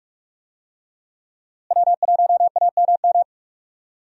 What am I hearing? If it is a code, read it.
Morse code, heard as a keyed tone, about 42 wpm, 700 Hz. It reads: W1AMM
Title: USA Call Signs - 5 Characters - Repeated 3x - 45wpm